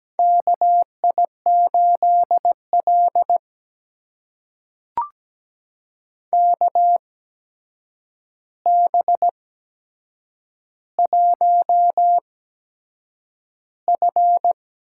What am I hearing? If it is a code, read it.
Morse code, heard as a keyed tone, about 17 wpm, spaced as for 7 wpm, 700 Hz. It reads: KI8L KB1F
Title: USA Call Signs - 4 Characters - 8x Character Spacing - 17wpm